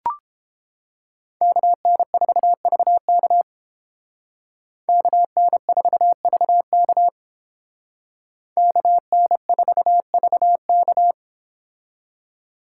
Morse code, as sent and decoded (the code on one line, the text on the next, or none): -.- -. ....- ...- -.- / -.- -. ....- ...- -.- / -.- -. ....- ...- -.-
KN4VK KN4VK KN4VK